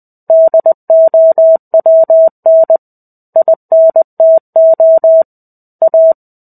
Morse code, heard as a keyed tone, about 20 wpm, 650 Hz. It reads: DOWN INTO A
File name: 05-Chapter-05 20 WPM